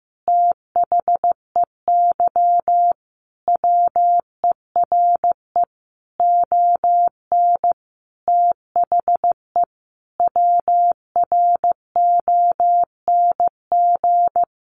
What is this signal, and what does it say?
Morse code, heard as a keyed tone, about 15 wpm, 700 Hz. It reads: THEY WERE ON THE WRONG
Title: Sentences from Top 900 Words 15wpm